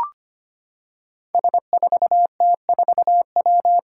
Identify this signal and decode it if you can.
Morse code, heard as a keyed tone, about 25 wpm, 700 Hz. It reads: S4T4W
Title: Five Challenging Letter-Number Combinations 25wpm